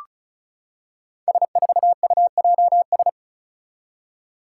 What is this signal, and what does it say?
Morse code, heard as a keyed tone, about 34 wpm, 700 Hz. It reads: S4UJS
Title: Five Challenging Letter-Number Combinations 35wpm